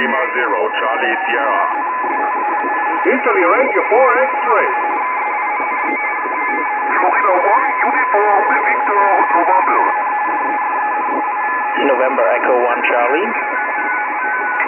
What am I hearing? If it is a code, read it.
Two-tone teleprinter signal, about 170 Hz shift, with speech.